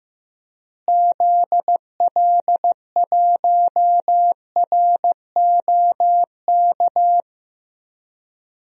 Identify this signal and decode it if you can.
Morse code, heard as a keyed tone, about 15 wpm, 700 Hz. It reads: ZL1ROK
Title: DXCC-100 Call Signs - 6 Characters - Repeated 3x - 15wpm